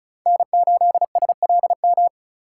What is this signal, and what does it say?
Morse code, heard as a keyed tone, about 36 wpm, 700 Hz. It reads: N8SLM
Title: USA Call Signs - 5 Characters - 3x Character Spacing - 35wpm